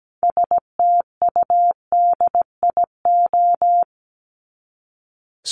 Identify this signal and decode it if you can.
Morse code, heard as a keyed tone, about 17 wpm, 700 Hz. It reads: STUDIO
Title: Top 1001-1250 Words 17wpm